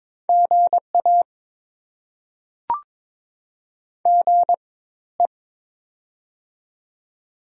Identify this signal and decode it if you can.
Morse code, heard as a keyed tone, about 22 wpm, spaced as for 14 wpm, 690 Hz. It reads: GA GE
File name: Top 20 QSO Elements - 4x Character Spacing - 22wpm